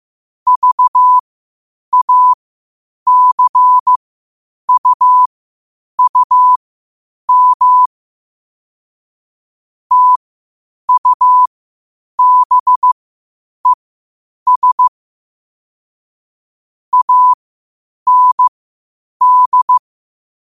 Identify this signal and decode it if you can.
Morse code, heard as a keyed tone, about 15 wpm, spaced as for 8 wpm, 1000 Hz. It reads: VACUUM TUBES AND
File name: qotdfiles / 8wpm